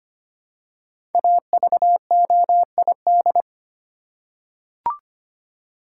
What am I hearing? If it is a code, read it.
Morse code, heard as a keyed tone, about 25 wpm, 700 Hz. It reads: AVOID
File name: 5 Letter Words - First 3 Letters Given - 25wpm